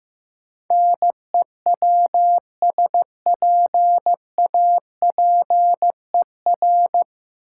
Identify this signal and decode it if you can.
Morse code, heard as a keyed tone, about 16 wpm, 700 Hz. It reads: NEWSPAPER
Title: Sentences from Top 1100 Words - New Word Drill Alt - 15wpm